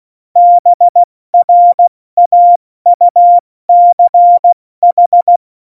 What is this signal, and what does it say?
Morse code, heard as a keyed tone, about 16 wpm, 700 Hz. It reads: BRAUCH
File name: Morsekurs / Witze / Witz 08. Dezember 2020, 70 F80